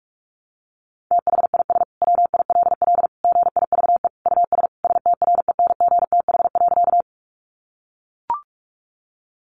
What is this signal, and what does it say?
Morse code, heard as a keyed tone, about 44 wpm, 700 Hz. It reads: THIS WILL GIVE US STRENGTH.
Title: Sentences from Top 1300 Words 45wpm